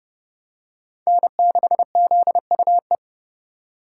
Morse code, heard as a keyed tone, about 28 wpm, 700 Hz. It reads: N6ZUE